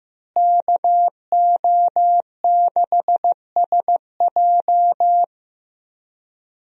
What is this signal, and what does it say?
Morse code, heard as a keyed tone, about 15 wpm, 700 Hz. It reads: KO6SJ
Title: USA Call Signs - 5 Characters - Repeated 2x - 15wpm